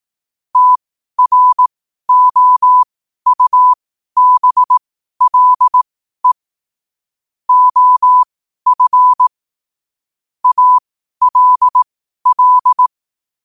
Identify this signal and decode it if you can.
Morse code, heard as a keyed tone, about 18 wpm, spaced as for 12 wpm, 1000 Hz. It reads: TROUBLE OF ALL
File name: qotdfiles / 12wpm